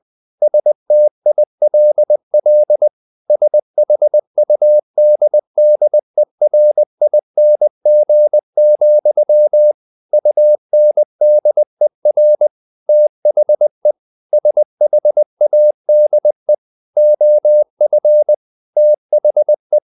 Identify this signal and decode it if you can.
Morse code, heard as a keyed tone, about 20 wpm, 600 Hz. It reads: STILL SHUDDERING, UNDER THE SHADE OF THE
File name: the old man and the sea / the old man and the sea part 1 20wpm